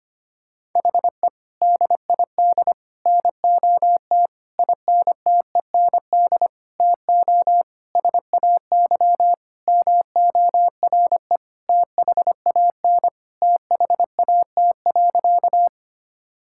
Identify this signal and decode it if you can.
Morse code, heard as a keyed tone, about 25 wpm, 700 Hz. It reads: HE DID NOT INTEND TO SAY MORE THAN THAT.
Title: Sentences from Top 1600 Words 25wpm